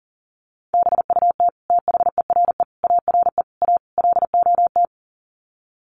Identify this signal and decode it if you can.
Morse code, heard as a keyed tone, about 40 wpm, 700 Hz. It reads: BUT THERE ARE A LOT
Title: Sets of 5 Words Encore2 40wpm